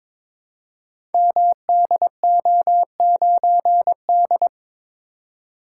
Morse code, heard as a keyed tone, about 22 wpm, 700 Hz. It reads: MDO9D
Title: Code Groups - 5 Letter:Numbers - 22wpm